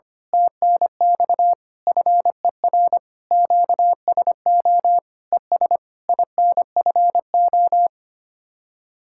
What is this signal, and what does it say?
Morse code, heard as a keyed tone, about 25 wpm, 700 Hz. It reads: TNX FER QSO ES INFO
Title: Speed-Racing Common QSO Phrases - 25wpm